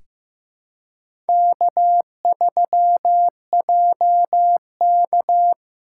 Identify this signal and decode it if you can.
Morse code, heard as a keyed tone, about 15 wpm, 700 Hz. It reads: K3JK
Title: USA Callsign DE Callsign - 4 Characters - 15wpm